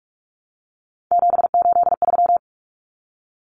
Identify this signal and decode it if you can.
Morse code, heard as a keyed tone, about 44 wpm, 700 Hz. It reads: 783
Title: Numbers - 3-Digit 45wpm